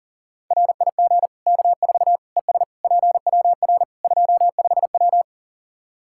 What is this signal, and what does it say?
Morse code, heard as a keyed tone, about 40 wpm, 700 Hz. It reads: RIG K4 ES PWR 25W
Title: Practice QSO - Exchange 2 - Info - 40wpm